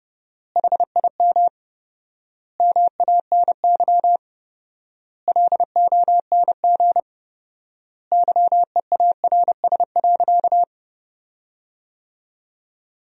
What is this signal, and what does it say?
Morse code, heard as a keyed tone, about 30 wpm, 700 Hz. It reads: HIM MANY LONG YEARS.
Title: 4x Word-Spacing - Sentences from Top 200 Words - 30wpm